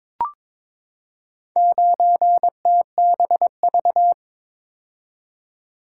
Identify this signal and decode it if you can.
Morse code, heard as a keyed tone, about 22 wpm, 700 Hz. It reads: 9TBV
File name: practice / 22wpm / Four Letter-Number Combinations - 22wpm